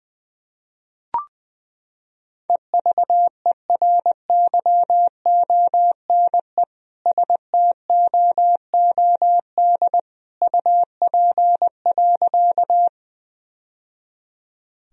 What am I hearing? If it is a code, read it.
Morse code, heard as a keyed tone, about 20 wpm, 700 Hz. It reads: EVERYONE STOOD UP.
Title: Sentences from Top 1000 Words - Review Part 2 - 20wpm